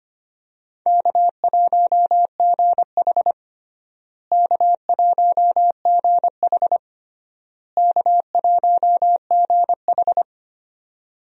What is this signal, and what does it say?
Morse code, heard as a keyed tone, about 25 wpm, 700 Hz. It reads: K1GH K1GH K1GH
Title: USA Call Signs - 4 Characters - Repeated 3x - 25wpm